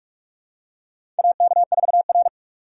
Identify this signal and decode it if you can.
Morse code, heard as a keyed tone, about 45 wpm, 700 Hz. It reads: AK4R
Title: USA Call Signs - 4 Characters - 3x Character Spacing - 45wpm